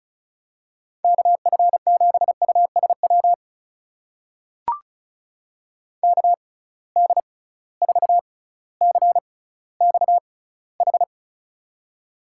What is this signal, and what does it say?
Morse code, heard as a keyed tone, about 36 wpm, spaced as for 18 wpm, 690 Hz. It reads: KF7USW KD4CXH
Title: USA Call Signs - 6 Characters - 6x Character Spacing - 35wpm